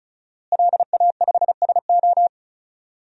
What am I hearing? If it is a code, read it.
Morse code, heard as a keyed tone, about 36 wpm, 700 Hz. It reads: LA5SO
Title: DXCC-100 Call Signs - 5 Characters - Repeated 2x - 35wpm